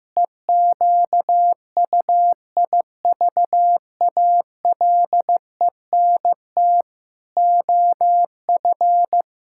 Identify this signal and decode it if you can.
Morse code, heard as a keyed tone, about 16 wpm, 700 Hz. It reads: EQUIVALENT OF
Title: Sentences from Top 1600 Words 15wpm